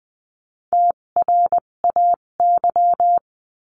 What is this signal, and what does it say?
Morse code, heard as a keyed tone, about 20 wpm, spaced as for 17 wpm, 700 Hz. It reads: TRAY
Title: 257 Lesson 42b - Words 14-20wpm